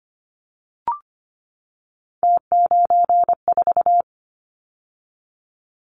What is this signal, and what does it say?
Morse code, heard as a keyed tone, about 25 wpm, 700 Hz. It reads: T94
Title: Callsign Prefixes for Top 50 DXCC - 25wpm